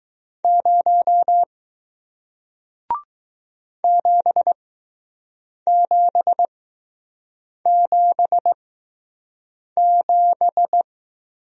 Morse code, heard as a keyed tone, about 20 wpm, spaced as for 6 wpm, 700 Hz. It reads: 0 7777